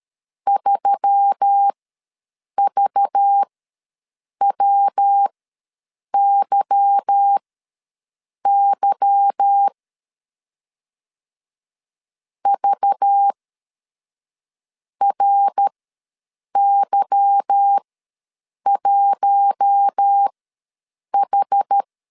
Morse code, heard as a keyed tone, about 13 wpm, spaced as for 6 wpm, 800 Hz. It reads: 3VWYY VRY1H
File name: LEZIONI / 8 YJ45 / 8 Y ws 12wpm 3409